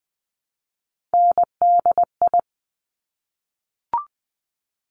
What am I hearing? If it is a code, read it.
Morse code, heard as a keyed tone, about 20 wpm, 700 Hz. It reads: NDI
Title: Top Three Letter Combinations in Words 20wpm